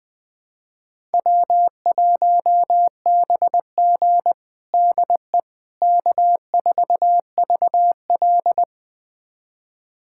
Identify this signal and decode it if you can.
Morse code, heard as a keyed tone, about 20 wpm, 700 Hz. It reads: W1BG DE K4VL
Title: USA Callsign DE Callsign - 4 Characters - 20wpm